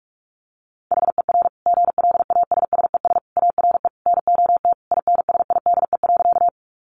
Morse code, heard as a keyed tone, about 44 wpm, 700 Hz. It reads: HER GLASSES ARE NOT INSIDE.
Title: Additional Easy Phrases 45wpm